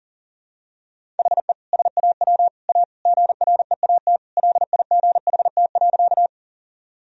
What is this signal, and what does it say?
Morse code, heard as a keyed tone, about 40 wpm, 700 Hz. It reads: HE SAW A GREAT LIGHT.
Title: Sentences from Top 1000 Words - Review Part 1 - 40wpm